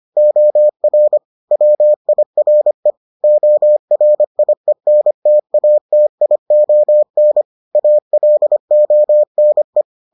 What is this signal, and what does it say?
Morse code, heard as a keyed tone, about 25 wpm, 600 Hz. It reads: OR WIRE ORIENTATION ALONE